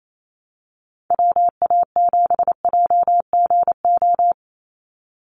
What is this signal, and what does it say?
Morse code, heard as a keyed tone, about 28 wpm, 700 Hz. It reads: WA7JGO
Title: USA Call Signs 28wpm